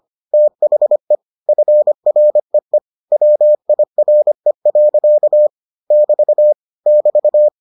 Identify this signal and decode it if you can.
Morse code, heard as a keyed tone, about 25 wpm, 600 Hz. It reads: THE FREE WIRE. <BT> <BT>